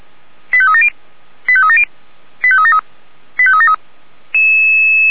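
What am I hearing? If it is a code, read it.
Five-tone selective call (ZVEI), heard as quick runs of five tones, tones of about 70 ms. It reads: 85370 85370 85373 85373
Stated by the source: Five-tone ZVEI selective call received by SDR.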